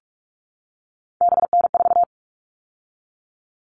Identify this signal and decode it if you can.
Morse code, heard as a keyed tone, about 44 wpm, 700 Hz. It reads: BN4
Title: All Cut Numbers - 3-Digit 45wpm